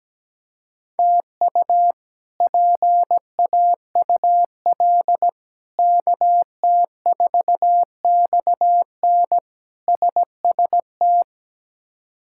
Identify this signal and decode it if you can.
Morse code, heard as a keyed tone, about 17 wpm, 700 Hz. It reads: TU PAUL KT4XN SST
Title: K1USN SST QSO Practice - 17wpm